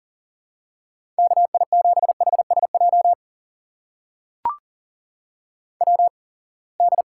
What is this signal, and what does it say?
Morse code, heard as a keyed tone, about 40 wpm, spaced as for 28 wpm, 700 Hz. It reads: KI7HSJ WD
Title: USA Call Signs - 6 Characters - 8x Character Spacing - 40wpm